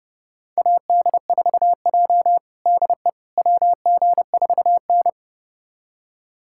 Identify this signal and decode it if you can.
Morse code, heard as a keyed tone, about 30 wpm, 700 Hz. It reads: AD4J DE WG4N
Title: USA Callsign DE Callsign - 4 Characters - 30wpm